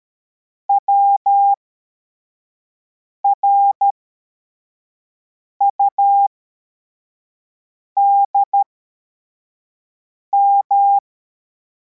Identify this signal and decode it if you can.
Morse code, heard as a keyed tone, about 13 wpm, spaced as for 4 wpm, 800 Hz. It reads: WRUDM